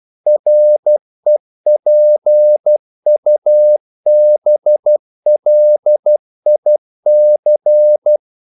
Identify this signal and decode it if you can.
Morse code, heard as a keyed tone, about 12 wpm, 600 Hz. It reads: REPUBLIC
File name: CW / Practice / NationalCapitals12